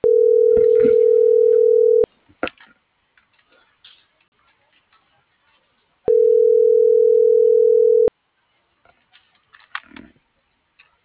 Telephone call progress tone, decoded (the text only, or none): ringback tone